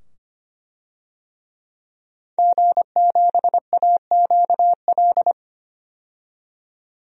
Morse code, heard as a keyed tone, about 25 wpm, 700 Hz. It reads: G7AQL
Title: DXCC-100 Call Signs - 5 Characters - 25wpm